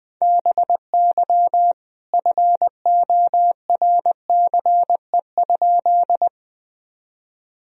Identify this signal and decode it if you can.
Morse code, heard as a keyed tone, about 20 wpm, 700 Hz. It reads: BY FORCE?
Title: Sentences from Top 1000 Words - Review Part 3 - 20wpm